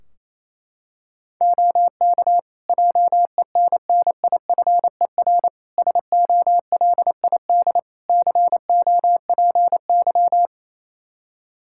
Morse code, heard as a keyed tone, about 28 wpm, 700 Hz. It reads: OK JENNIFER SOLID COPY
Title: Practice QSO - Exchange 3 - More Info - 28wpm